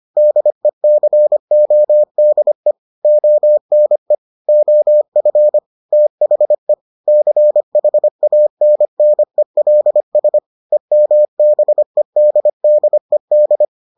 Morse code, heard as a keyed tone, about 25 wpm, 600 Hz. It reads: DECODE ONE OF THE CHANNELS EMBEDDED